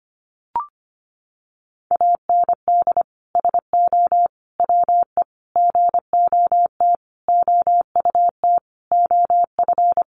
Morse code, heard as a keyed tone, about 25 wpm, 700 Hz. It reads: AND SO WE GOT OUT OF
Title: Sentences from Top 1500 Words 25wpm